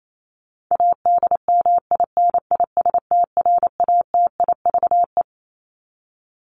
Morse code, heard as a keyed tone, about 28 wpm, 700 Hz. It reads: ADMINISTRATIVE
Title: Top 1501-1750 Words 28wpm